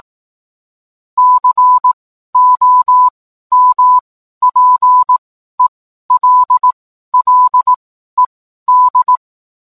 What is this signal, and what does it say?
Morse code, heard as a keyed tone, about 18 wpm, spaced as for 12 wpm, 1000 Hz. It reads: COMPELLED